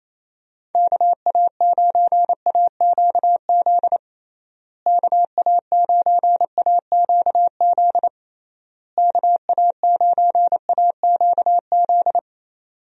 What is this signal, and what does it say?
Morse code, heard as a keyed tone, about 28 wpm, 700 Hz. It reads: KA9AQZ KA9AQZ KA9AQZ
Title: USA Call Signs - 6 Characters - Repeated 3x - 28wpm